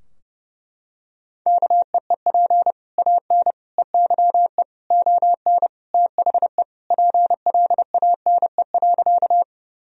Morse code, heard as a keyed tone, about 30 wpm, 700 Hz. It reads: KEEP AN EYE ON THE PLANE.
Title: Sentences from Top 1000 Words - Review Part 3 - 30wpm